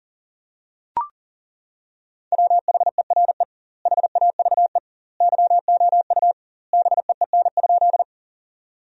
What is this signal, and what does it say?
Morse code, heard as a keyed tone, about 40 wpm, 700 Hz. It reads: WHERE HAVE YOU BEEN?
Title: 2x Word-Spacing - Sentences from Top 200 Words - 40wpm